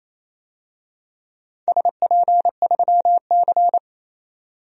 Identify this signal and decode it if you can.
Morse code, heard as a keyed tone, about 28 wpm, 700 Hz. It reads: SP3C
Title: DXCC-100 Call Signs - 4 Characters - 28wpm